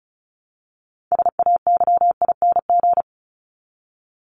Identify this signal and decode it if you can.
Morse code, heard as a keyed tone, about 34 wpm, 700 Hz. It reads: SAYING
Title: ING Suffix - Top 100 Words with Base Words 35wpm